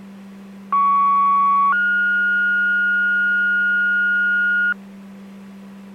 Two-tone page, heard as a one-second tone, then a three-second tone, about 1120 then 1430 Hz, with noise. It is heard from 0.7 to 4.7 s.